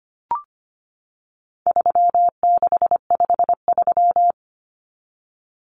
Morse code, heard as a keyed tone, about 25 wpm, 700 Hz. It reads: 3653
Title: Common Cut Numbers - 4-Digit 25wpm